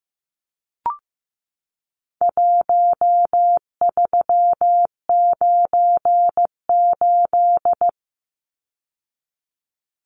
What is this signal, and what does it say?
Morse code, heard as a keyed tone, about 15 wpm, 700 Hz. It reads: 1398
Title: Numbers - Contest Style - 4-Digit 15wpm